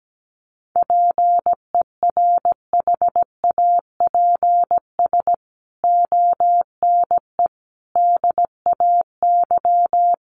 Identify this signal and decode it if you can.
Morse code, heard as a keyed tone, about 17 wpm, 700 Hz. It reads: PERHAPS ONE DAY
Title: Sentences from Top 600 Words 17wpm